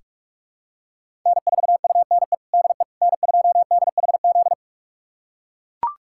Morse code, heard as a keyed tone, about 43 wpm, 700 Hz. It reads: N4UNE DE N2DHZ
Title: USA Callsign DE Callsign - 5 Characters - 45wpm